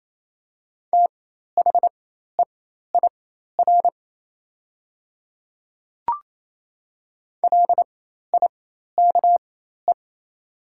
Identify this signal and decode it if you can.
Morse code, heard as a keyed tone, about 28 wpm, spaced as for 13 wpm, 700 Hz. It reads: THEIR LIKE
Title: 4x Character Spacing - Top 200 Words - 28wpm